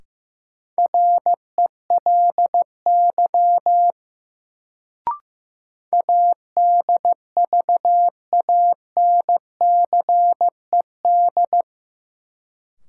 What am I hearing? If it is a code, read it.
Morse code, heard as a keyed tone, about 15 wpm, 700 Hz. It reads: RELY ADVANCED
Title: Sentences from Top 1200 Words - New Word Drill - 15wpm